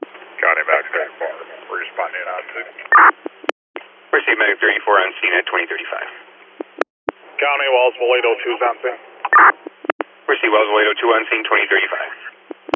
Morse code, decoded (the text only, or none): none